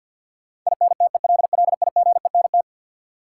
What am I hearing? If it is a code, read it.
Morse code, heard as a keyed tone, about 49 wpm, 700 Hz. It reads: INTELLIGENT
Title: Sentences from Top 1000 Words - Review Part 7 - New Word Drill - 50wpm